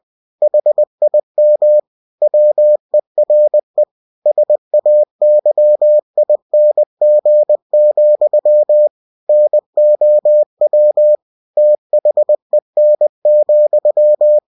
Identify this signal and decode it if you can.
Morse code, heard as a keyed tone, about 20 wpm, 600 Hz. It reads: HIM WERE SAYING, NOW THEN,
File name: cw-books / winnie-the-pooh / winnie-20wpm-page-0049